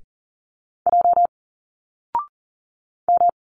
Morse code, heard as a keyed tone, about 40 wpm, 700 Hz. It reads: J M
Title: Single Letters 40wpm